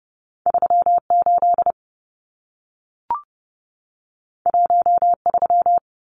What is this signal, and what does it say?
Morse code, heard as a keyed tone, about 30 wpm, 700 Hz. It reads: 38 13
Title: Numbers - 2-Digit 30wpm